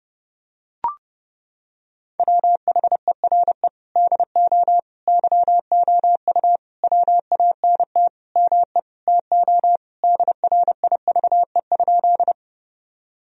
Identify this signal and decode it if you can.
Morse code, heard as a keyed tone, about 30 wpm, 700 Hz. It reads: WHERE DO YOU WANT ME TO DRIVE?
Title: Sentences from Top 1000 Words - Review Part 2 - 30wpm